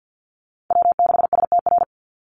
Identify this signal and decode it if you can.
Morse code, heard as a keyed tone, about 49 wpm, 700 Hz. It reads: W6STR
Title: Call Signs - 50wpm